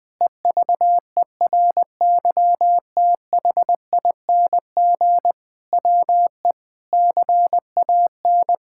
Morse code, heard as a keyed tone, about 20 wpm, 700 Hz. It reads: EVERYTHING WE CAN
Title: Sentences with 6-Word Phrases - Spoken Beginning - 20wpm